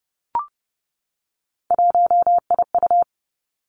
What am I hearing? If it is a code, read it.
Morse code, heard as a keyed tone, about 30 wpm, 700 Hz. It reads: 1IU
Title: Three Challenging Letter-Number Combinations 30wpm